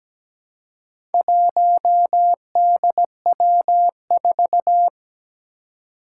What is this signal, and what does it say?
Morse code, heard as a keyed tone, about 17 wpm, 700 Hz. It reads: 1DW4
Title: Four Challenging Letter-Number Combinations 17wpm